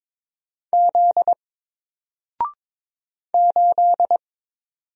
Morse code, heard as a keyed tone, about 22 wpm, 700 Hz. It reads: Z 8
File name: Single-Letter-Number-22wpm